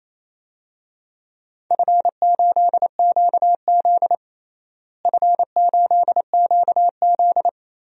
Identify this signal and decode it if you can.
Morse code, heard as a keyed tone, about 28 wpm, 700 Hz. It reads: F8QZ F8QZ
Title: DXCC-100 Call Signs - 4 Characters - Repeated 2x - 28wpm